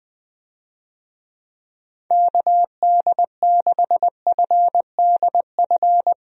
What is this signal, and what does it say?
Morse code, heard as a keyed tone, about 20 wpm, 700 Hz. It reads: KD6FDF
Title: USA Call Signs - 6 Characters - 20wpm